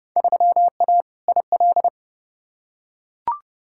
Morse code, heard as a keyed tone, about 32 wpm, 700 Hz. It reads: 3A IL
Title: Field Day Exchanges - 30wpm